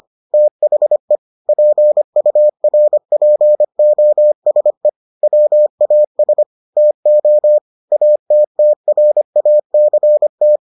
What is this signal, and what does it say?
Morse code, heard as a keyed tone, about 25 wpm, 600 Hz. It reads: THE PURPOSE WAS TO ATTRACT